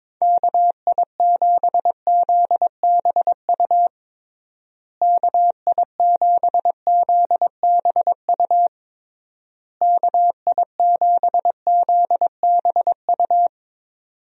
Morse code, heard as a keyed tone, about 22 wpm, 700 Hz. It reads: KI7ZBU KI7ZBU KI7ZBU
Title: USA Call Signs - 6 Characters - Repeated 3x - 22wpm